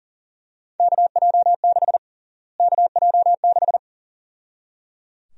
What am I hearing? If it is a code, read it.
Morse code, heard as a keyed tone, about 37 wpm, 700 Hz. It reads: KJ6 KJ6
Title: USA Call Sign Prefixes - Repeated 2x - 40wpm